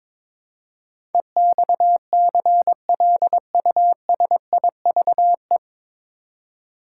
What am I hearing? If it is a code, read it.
Morse code, heard as a keyed tone, about 22 wpm, 700 Hz. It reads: EXCLUSIVE
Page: Morse Code Ninja practice 22wpm